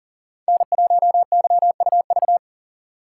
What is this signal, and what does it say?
Morse code, heard as a keyed tone, about 40 wpm, 700 Hz. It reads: N1YUV
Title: USA Call Signs - 5 Characters - 6x Character Spacing - 40wpm